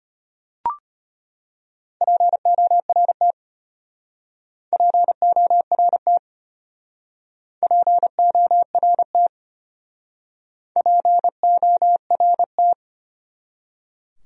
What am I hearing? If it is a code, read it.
Morse code, heard as a keyed tone, about 31 wpm, 700 Hz. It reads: PORT PORT PORT PORT